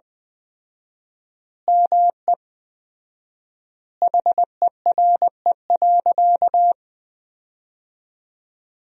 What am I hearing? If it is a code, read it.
Morse code, heard as a keyed tone, about 20 wpm, 700 Hz. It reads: ME HERE.
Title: 4x Word-Spacing - Sentences from Top 200 Words - 20wpm